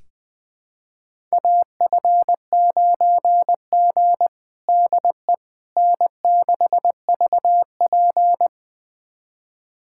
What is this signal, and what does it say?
Morse code, heard as a keyed tone, about 21 wpm, 700 Hz. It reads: AF9G DE N6VP
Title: USA Callsign DE Callsign - 4 Characters - 20wpm